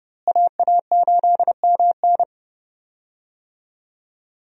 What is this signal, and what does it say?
Morse code, heard as a keyed tone, about 31 wpm, 700 Hz. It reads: AA8MN
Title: USA Call Signs - 5 Characters - Repeated 3x - 30wpm